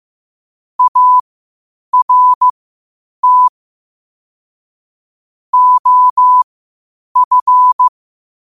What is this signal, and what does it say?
Morse code, heard as a keyed tone, about 15 wpm, spaced as for 8 wpm, 1000 Hz. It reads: ART OF
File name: qotdfiles / 8wpm